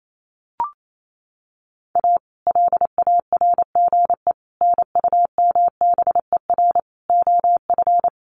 Morse code, heard as a keyed tone, about 28 wpm, 700 Hz. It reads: A LARGE NUMBER OF
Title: Sets of 4 Words 28wpm